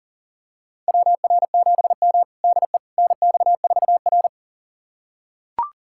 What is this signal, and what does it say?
Morse code, heard as a keyed tone, about 40 wpm, 700 Hz. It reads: WR7M DE NX4R
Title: USA Callsign DE Callsign - 4 Characters - 40wpm